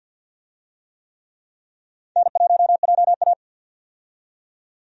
Morse code, heard as a keyed tone, about 51 wpm, 700 Hz. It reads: N1JA
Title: USA Call Signs - 4 Characters - 50wpm